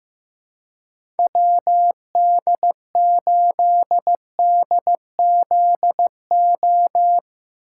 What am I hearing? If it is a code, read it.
Morse code, heard as a keyed tone, about 16 wpm, 700 Hz. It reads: WD8DZO
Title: USA Call Signs - 6 Characters - 4x Character Spacing - 15wpm